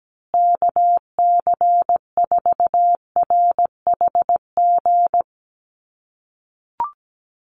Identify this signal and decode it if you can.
Morse code, heard as a keyed tone, about 17 wpm, 700 Hz. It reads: KC4RHG
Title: USA Call Signs Repeated 2x 17wpm